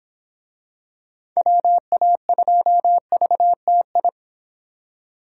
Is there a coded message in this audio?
Yes: Morse code from 1.4 to 4.1 s, heard as a keyed tone, about 26 wpm, 700 Hz. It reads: WA2VTI